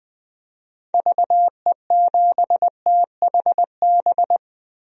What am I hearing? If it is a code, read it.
Morse code, heard as a keyed tone, about 20 wpm, 700 Hz. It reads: VE7THB
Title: DXCC-100 Call Signs - 6 Characters - Repeated 2x - 20wpm